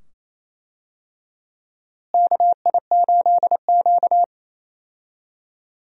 Morse code, heard as a keyed tone, about 28 wpm, 700 Hz. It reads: KI8Q
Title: USA Call Signs - 4 Characters - Repeated 2x - 28wpm